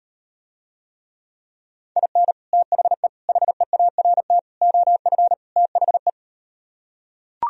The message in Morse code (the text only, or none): IN THE HEART OF THE